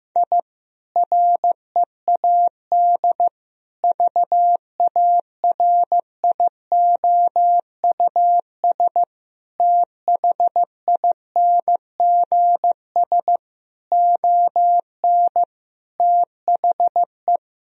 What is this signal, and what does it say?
Morse code, heard as a keyed tone, about 15 wpm, 700 Hz. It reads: I READ VARIOUS THINGS ON THE
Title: Sentences from Top 1000 Words - Review Part 4 - 15wpm